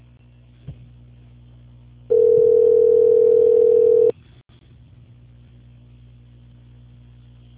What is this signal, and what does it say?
Ringback tone (North America), 440 + 480 Hz.
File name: out- -2037- - - .24756